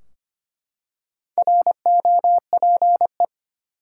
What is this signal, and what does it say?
Morse code, heard as a keyed tone, about 25 wpm, 700 Hz. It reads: ROPE
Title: Four Letter Words - 25wpm